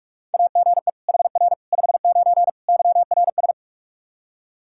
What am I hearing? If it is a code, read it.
Morse code, heard as a keyed tone, about 41 wpm, 700 Hz. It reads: AGE HR 59 YRS
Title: K1USN SST QSO Practice 45wpm